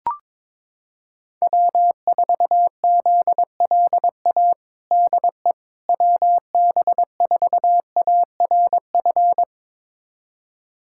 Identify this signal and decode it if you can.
Morse code, heard as a keyed tone, about 22 wpm, 700 Hz. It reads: W4ZLA DE WB4ARF
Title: USA Callsign DE Callsign 22wpm